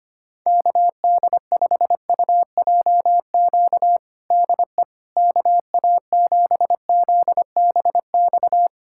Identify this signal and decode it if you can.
Morse code, heard as a keyed tone, about 25 wpm, 700 Hz. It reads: KD5UJQ DE KA7ZBX
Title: USA Callsign DE Callsign - 6 Characters - 25wpm